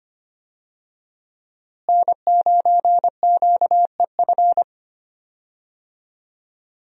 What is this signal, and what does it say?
Morse code, heard as a keyed tone, about 25 wpm, 700 Hz. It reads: N9QEF
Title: USA Call Signs - 5 Characters - Repeated 2x - 25wpm